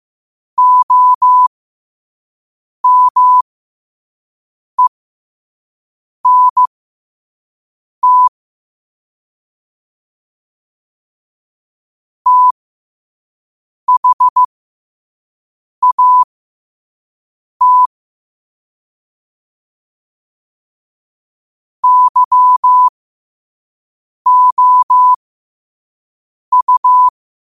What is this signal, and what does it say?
Morse code, heard as a keyed tone, about 15 wpm, spaced as for 5 wpm, 1000 Hz. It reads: OMENT THAT YOU
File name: qotdfiles / 5wpm